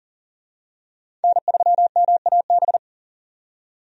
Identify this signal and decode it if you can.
Morse code, heard as a keyed tone, about 40 wpm, 700 Hz. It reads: N3MAB
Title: USA Call Signs - 5 Characters - 40wpm